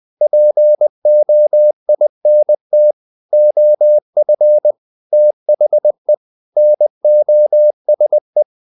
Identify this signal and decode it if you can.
Morse code, heard as a keyed tone, about 20 wpm, 600 Hz. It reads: POINT OF THE NOSE
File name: the old man and the sea part 2 20wpm